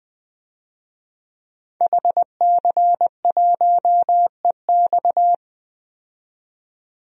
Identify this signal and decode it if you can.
Morse code, heard as a keyed tone, about 20 wpm, 700 Hz. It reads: HC1EX
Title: DXCC-100 Call Signs - 5 Characters - 20wpm